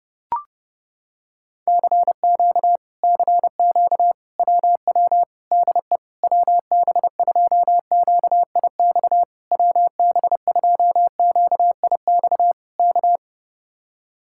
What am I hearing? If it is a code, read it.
Morse code, heard as a keyed tone, about 31 wpm, 700 Hz. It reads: CQ CQ WW DE WB2QIX WB2QIX K